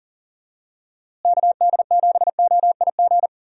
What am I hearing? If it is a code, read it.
Morse code, heard as a keyed tone, about 40 wpm, 700 Hz. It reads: KD7OIG
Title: USA Call Signs - 6 Characters - Repeated 3x - 40wpm